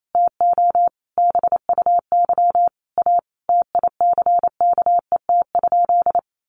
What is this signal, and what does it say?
Morse code, heard as a keyed tone, about 28 wpm, 700 Hz. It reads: TO BUY A TICKET?
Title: Sentences from Top 1400 Words 28wpm